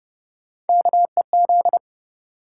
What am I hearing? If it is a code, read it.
Morse code, heard as a keyed tone, about 28 wpm, 700 Hz. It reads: KEZ